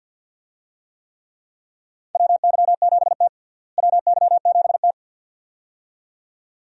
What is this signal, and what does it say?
Morse code, heard as a keyed tone, about 46 wpm, 700 Hz. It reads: WY7T WY7T
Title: USA Call Signs - 4 Characters - Repeated 2x - 50wpm